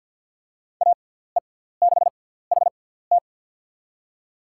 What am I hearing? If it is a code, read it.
Morse code, heard as a keyed tone, about 48 wpm, spaced as for 17 wpm, 700 Hz. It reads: AE6HT
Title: USA Call Signs - 5 Characters - 6x Character Spacing - 50wpm